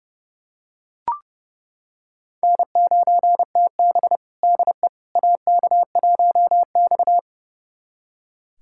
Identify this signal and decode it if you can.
Morse code, heard as a keyed tone, about 30 wpm, 700 Hz. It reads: N9TB DE AK1X
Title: USA Callsign DE Callsign - 4 Characters - 30wpm